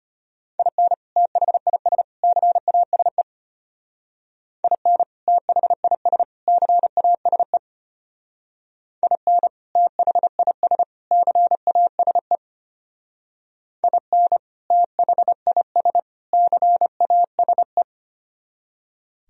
Morse code, heard as a keyed tone, about 33 wpm, 700 Hz. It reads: IN THIS CASE IN THIS CASE IN THIS CASE IN THIS CASE